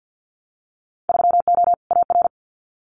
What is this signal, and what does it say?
Morse code, heard as a keyed tone, about 50 wpm, 700 Hz. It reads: 3O AR
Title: Winter Field Day Exchanges Encore 50wpm